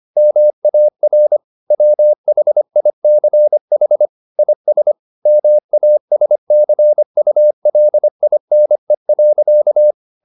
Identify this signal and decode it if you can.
Morse code, heard as a keyed tone, about 25 wpm, 600 Hz. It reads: MAR WHICH IS MASCULINE.
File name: the old man and the sea part 1 25wpm